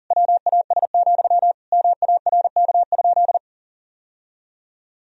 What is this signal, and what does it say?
Morse code, heard as a keyed tone, about 40 wpm, 700 Hz. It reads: WAS, MARK?
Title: Sentences from Top 1000 Words - Review Part 1 - 40wpm